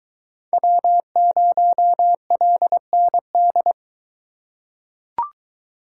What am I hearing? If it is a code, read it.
Morse code, heard as a keyed tone, about 21 wpm, 700 Hz. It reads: W0LND